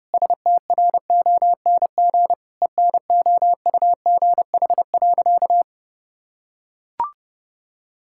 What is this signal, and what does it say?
Morse code, heard as a keyed tone, about 30 wpm, 700 Hz. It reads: STRONG ENOUGH.
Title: Sentences from Top 1600 Words 30wpm